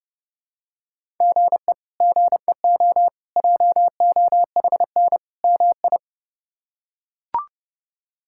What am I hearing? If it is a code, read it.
Morse code, heard as a keyed tone, about 30 wpm, 700 Hz. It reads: GE GEO JOHN MI
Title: K1USN SST QSO Practice - 30wpm